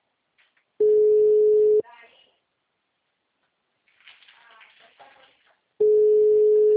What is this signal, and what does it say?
Ringback tone (Europe, 425 Hz), with speech.
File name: out- -2020- - - .18153